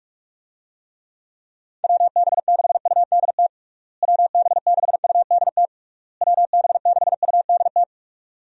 Morse code, heard as a keyed tone, about 46 wpm, 700 Hz. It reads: WB6UDT WB6UDT WB6UDT
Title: USA Call Signs - 6 Characters - Repeated 3x - 45wpm